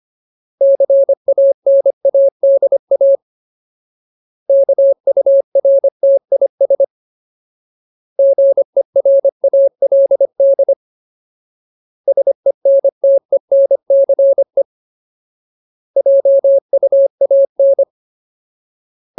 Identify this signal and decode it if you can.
Morse code, heard as a keyed tone, about 25 wpm, 550 Hz. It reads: CANADA KURTIS GERALD SENTENCE JUAN
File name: Ultimate Word List / 25WPM Ultimate